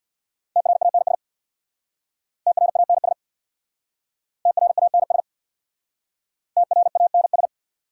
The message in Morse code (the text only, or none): TRANS TRANS TRANS TRANS